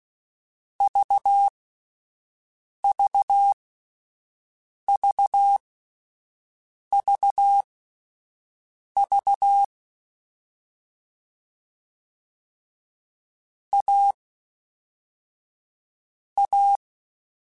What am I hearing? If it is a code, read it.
Morse code, heard as a keyed tone, about 16 wpm, spaced as for 5 wpm, 800 Hz. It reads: VVVVV A A
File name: Difficult Combos / Difficult Combos-mp3s / AUV AUV4 @5